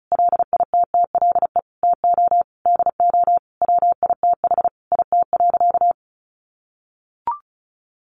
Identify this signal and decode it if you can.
Morse code, heard as a keyed tone, about 34 wpm, 700 Hz. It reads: LITTLE TO DO WITH IT.
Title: Sentences from Top 1500 Words 35wpm